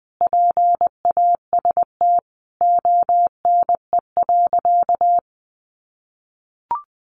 Morse code, heard as a keyed tone, about 20 wpm, 700 Hz. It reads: PAST ONE.
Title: Sentences from Top 1300 Words 20wpm